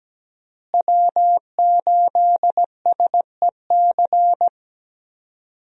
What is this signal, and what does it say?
Morse code, heard as a keyed tone, about 17 wpm, 700 Hz. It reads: W8SEC
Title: USA Call Signs - 5 Characters - Repeated 2x - 17wpm